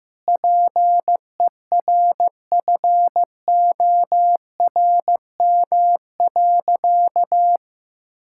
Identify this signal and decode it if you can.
Morse code, heard as a keyed tone, about 15 wpm, 700 Hz. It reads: PERFORM.
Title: Sentences from Top 1000 Words - Review Part 5 - 15wpm